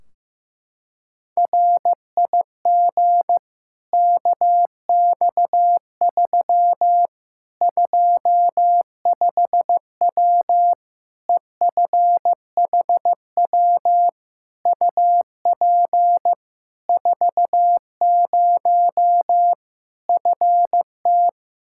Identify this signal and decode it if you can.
Morse code, heard as a keyed tone, about 15 wpm, 700 Hz. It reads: RIG KX3 25W EFHW UP 40 FT
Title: Practice QSO - Exchange 2 - Info - 15wpm